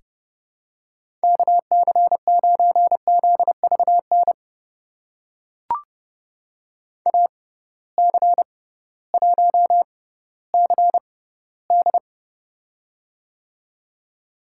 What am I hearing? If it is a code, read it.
Morse code, heard as a keyed tone, about 32 wpm, spaced as for 16 wpm, 700 Hz. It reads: KC9ZVN AC1CD
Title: USA Call Signs - 6x Character Spacing - 30wpm